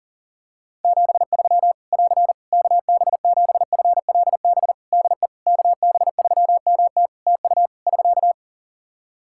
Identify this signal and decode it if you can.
Morse code, heard as a keyed tone, about 40 wpm, 700 Hz. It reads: 73 <AR> KB7FLB DE KB3MT TU <SK>
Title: Practice QSO - Exchange 4 - Ending - 40wpm